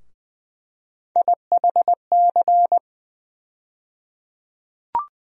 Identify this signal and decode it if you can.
Morse code, heard as a keyed tone, about 20 wpm, 700 Hz. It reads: IHC
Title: USA Call Sign Postfixes - 3 Letters 20wpm